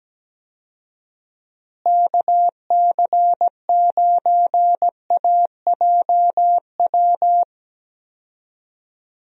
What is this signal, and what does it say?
Morse code, heard as a keyed tone, about 17 wpm, 700 Hz. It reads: KC9AJW
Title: USA Call Signs - 6 Characters - Repeated 3x - 17wpm